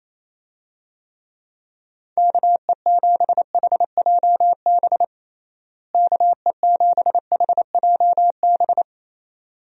Morse code, heard as a keyed tone, about 27 wpm, 700 Hz. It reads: KE7HJB KE7HJB
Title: USA Call Signs - 6 Characters - Repeated 2x - 28wpm